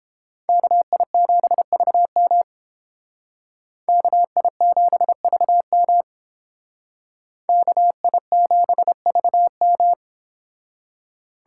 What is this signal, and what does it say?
Morse code, heard as a keyed tone, about 30 wpm, 700 Hz. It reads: KI7VM KI7VM KI7VM